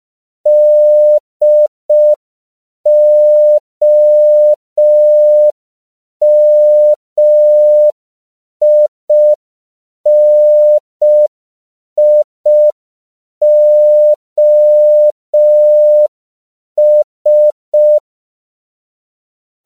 Morse code, heard as a keyed tone, about 5 wpm, 600 Hz. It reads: DOMINIOS